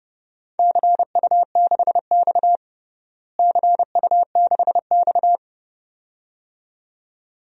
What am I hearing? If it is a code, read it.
Morse code, heard as a keyed tone, about 30 wpm, 700 Hz. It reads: CU6X CU6X
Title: DXCC-100 Call Signs - 4 Characters - Repeated 2x - 30wpm